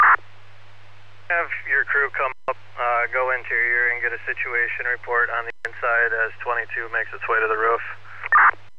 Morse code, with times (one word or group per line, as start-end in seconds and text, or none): none